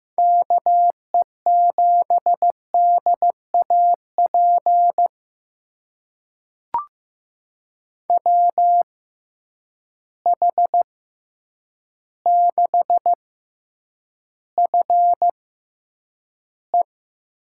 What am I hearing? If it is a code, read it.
Morse code, heard as a keyed tone, about 15 wpm, spaced as for 8 wpm, 700 Hz. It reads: KE7DAP WH6FE
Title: USA Call Signs - 6 Characters - 6x Character Spacing - 15wpm